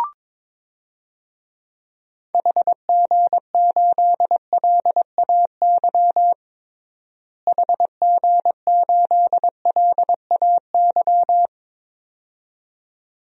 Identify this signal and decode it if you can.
Morse code, heard as a keyed tone, about 22 wpm, 700 Hz. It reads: HG8LAY HG8LAY
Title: DXCC-100 Call Signs - 6 Characters - Repeated 2x - 22wpm